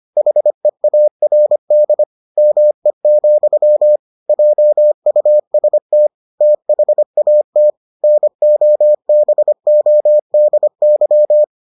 Morse code, heard as a keyed tone, about 25 wpm, 600 Hz. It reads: HEARD ME, JUST THAT NOBODY